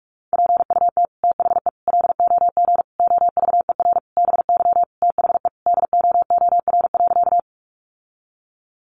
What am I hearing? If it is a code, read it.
Morse code, heard as a keyed tone, about 44 wpm, 700 Hz. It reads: PUT THE LOG OVER BY THE DOOR.
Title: Sentences from Top 1400 Words 45wpm